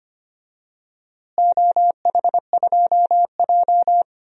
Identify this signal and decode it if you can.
Morse code, heard as a keyed tone, about 25 wpm, 700 Hz. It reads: OH2J
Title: DXCC-100 Call Signs - 4 Characters - Repeated 2x - 25wpm